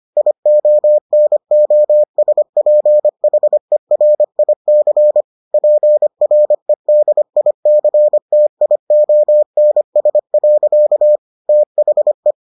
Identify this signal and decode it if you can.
Morse code, heard as a keyed tone, about 25 wpm, 600 Hz. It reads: IONOSPHERIC PREDICTIONS. THE